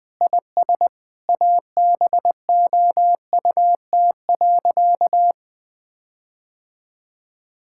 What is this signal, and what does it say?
Morse code, heard as a keyed tone, about 21 wpm, 700 Hz. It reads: IS ABOUT.
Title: Sentences with 8-Word Phrases - 20wpm